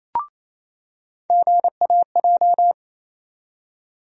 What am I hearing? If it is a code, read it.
Morse code, heard as a keyed tone, about 28 wpm, 700 Hz. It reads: GAJ
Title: USA Call Sign Postfixes - 3 Letters 28wpm